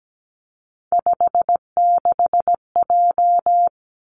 Morse code, heard as a keyed tone, about 17 wpm, 700 Hz. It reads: 56J
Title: Three Challenging Letter-Number Combinations 17wpm